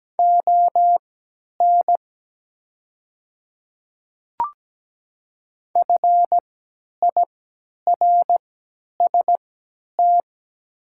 Morse code, heard as a keyed tone, about 17 wpm, spaced as for 10 wpm, 700 Hz. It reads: ON FIRST
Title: 3x Character Spacing - Top 200 Words - 17wpm